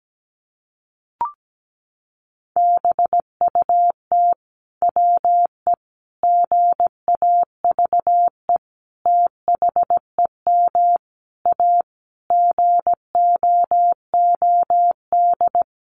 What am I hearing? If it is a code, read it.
Morse code, heard as a keyed tone, about 17 wpm, 700 Hz. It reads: BUT WE GAVE THEM A GOOD
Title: Sentences from Top 1200 Words 17wpm